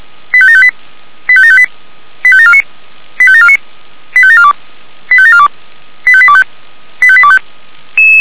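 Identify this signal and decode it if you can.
Five-tone selective call (ZVEI), heard as quick runs of five tones, tones of about 70 ms. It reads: 85758 85758 85749 85749 85742 85742 85825 85825
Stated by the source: ZVEI five-tone call 85827, received via SDR.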